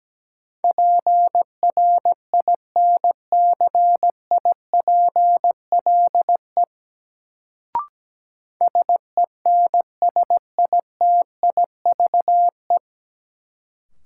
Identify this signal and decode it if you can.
Morse code, heard as a keyed tone, about 17 wpm, 700 Hz. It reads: PRINCIPLE SENSITIVE
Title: Sentences from Top 1400 Words - New Word Drill - 17wpm